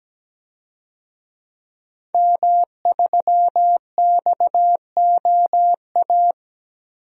Morse code, heard as a keyed tone, about 17 wpm, 700 Hz. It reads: M3XOA
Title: DXCC-100 Call Signs - 5 Characters - 17wpm